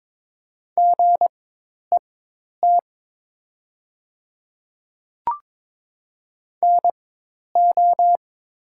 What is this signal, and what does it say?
Morse code, heard as a keyed tone, about 22 wpm, spaced as for 10 wpm, 700 Hz. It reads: GET NO
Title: 4x Character Spacing - Top 200 Words - 22wpm